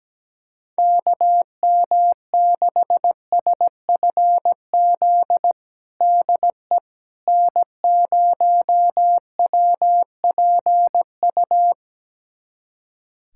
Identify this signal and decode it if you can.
Morse code, heard as a keyed tone, about 17 wpm, 700 Hz. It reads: KM6SFZ DE N0WPU
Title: USA Callsign DE Callsign 17wpm